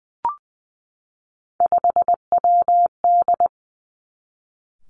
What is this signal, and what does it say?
Morse code, heard as a keyed tone, about 20 wpm, 700 Hz. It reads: 5WD
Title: Three Challenging Letter-Number Combinations 20wpm